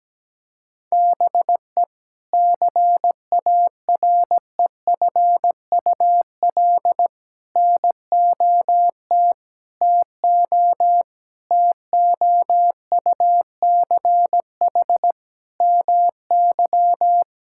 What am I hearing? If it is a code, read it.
Morse code, heard as a keyed tone, about 17 wpm, 700 Hz. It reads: BE CAREFUL NOT TO TOUCH MY
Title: Sentences from Top 1600 Words 17wpm